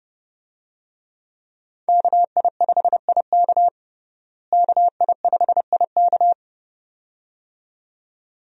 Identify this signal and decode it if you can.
Morse code, heard as a keyed tone, about 32 wpm, 700 Hz. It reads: KI5IK KI5IK
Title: USA Call Signs - 5 Characters - Repeated 2x - 30wpm